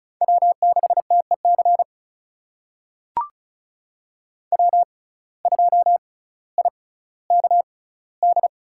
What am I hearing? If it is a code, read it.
Morse code, heard as a keyed tone, about 36 wpm, spaced as for 18 wpm, 700 Hz. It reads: W6TEC W2IKD
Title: USA Call Signs - 5 Characters - 6x Character Spacing - 35wpm